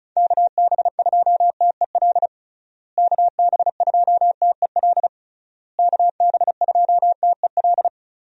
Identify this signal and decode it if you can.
Morse code, heard as a keyed tone, about 34 wpm, 700 Hz. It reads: KB2TEL KB2TEL KB2TEL
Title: USA Call Signs - 6 Characters - Repeated 3x - 35wpm